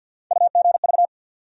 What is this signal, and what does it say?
Morse code, heard as a keyed tone, about 47 wpm, 700 Hz. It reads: UGV
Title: USA Call Sign Postfixes - 3 Letters 50wpm